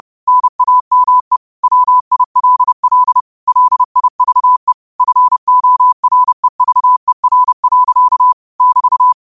Morse code, heard as a keyed tone, about 30 wpm, 1000 Hz. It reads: NAME WILL LIVE FOREVER. <BT>